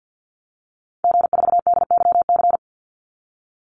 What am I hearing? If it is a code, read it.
Morse code, heard as a keyed tone, about 50 wpm, 700 Hz. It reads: G4DYC
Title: Call Signs - 50wpm